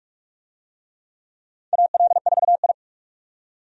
Morse code, heard as a keyed tone, about 43 wpm, 700 Hz. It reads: AL4I
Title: USA Call Signs - 4 Characters - 45wpm